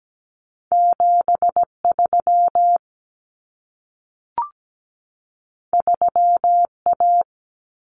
Morse code, heard as a keyed tone, about 17 wpm, 700 Hz. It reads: 73 3A
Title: Common Cut Numbers - 2-Digit 17wpm